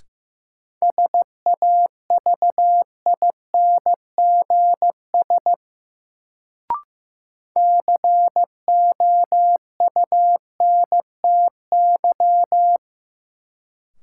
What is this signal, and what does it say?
Morse code, heard as a keyed tone, about 15 wpm, 700 Hz. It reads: SAVINGS COUNTY
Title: Sentences from Top 1200 Words - New Word Drill Alt - 15wpm